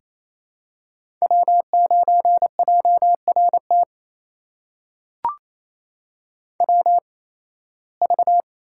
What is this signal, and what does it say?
Morse code, heard as a keyed tone, about 28 wpm, spaced as for 18 wpm, 700 Hz. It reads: W9JRT WV
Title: USA Call Signs - 5 Characters - 8x Character Spacing - 28wpm